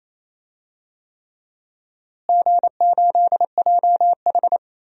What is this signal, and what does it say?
Morse code, heard as a keyed tone, about 28 wpm, 700 Hz. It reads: G8JH
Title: DXCC-100 Call Signs - 4 Characters - Repeated 3x - 28wpm